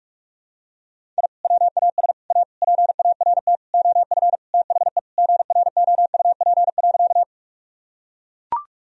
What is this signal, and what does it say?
Morse code, heard as a keyed tone, about 42 wpm, 700 Hz. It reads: I WAS A PART OF THE GROUP.
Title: Sentences from Top 1000 Words - Review Part 1 - 45wpm